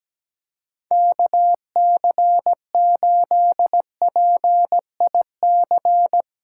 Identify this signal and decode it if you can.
Morse code, heard as a keyed tone, about 17 wpm, 700 Hz. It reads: KC8PIC
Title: USA Callsign DE Callsign - 6 Characters - 17wpm